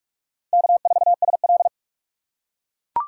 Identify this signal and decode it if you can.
Morse code, heard as a keyed tone, about 41 wpm, 700 Hz. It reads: K4SL